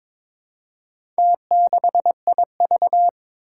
Morse code, heard as a keyed tone, about 22 wpm, 700 Hz. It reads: T6IV
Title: Four Challenging Letter-Number Combinations 22wpm